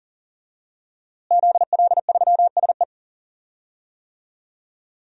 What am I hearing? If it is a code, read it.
Morse code, heard as a keyed tone, about 40 wpm, 700 Hz. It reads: ZL3SE
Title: DXCC-100 Call Signs - 5 Characters - Repeated 2x - 40wpm